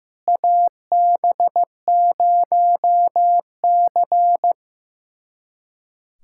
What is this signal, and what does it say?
Morse code, heard as a keyed tone, about 15 wpm, 700 Hz. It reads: AB0C
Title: USA Callsign DE Callsign - 4 Characters - 15wpm